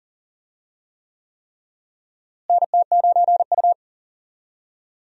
Morse code, heard as a keyed tone, about 40 wpm, 700 Hz. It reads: NT9U
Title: USA Call Signs - 4 Characters - Repeated 3x - 40wpm